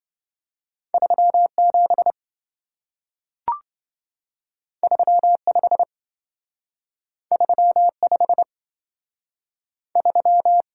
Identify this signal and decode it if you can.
Morse code, heard as a keyed tone, about 27 wpm, 700 Hz. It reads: 37 35 35 3